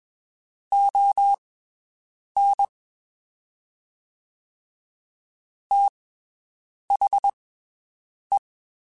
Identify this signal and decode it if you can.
Morse code, heard as a keyed tone, about 21 wpm, spaced as for 7 wpm, 800 Hz. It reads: ON THE
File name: (spaced) / QSOs Spaced-mp3s / WD0FBT 1 @20 sp